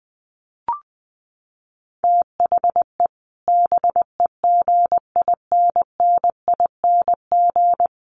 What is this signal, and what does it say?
Morse code, heard as a keyed tone, about 19 wpm, 700 Hz. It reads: THE BEGINNING
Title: Sets of 5 Words Encore2 20wpm